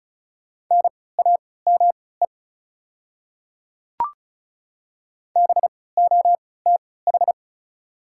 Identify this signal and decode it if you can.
Morse code, heard as a keyed tone, about 34 wpm, spaced as for 20 wpm, 700 Hz. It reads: NAME BOTH
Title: 3x Character Spacing - Top 300 Words - 35wpm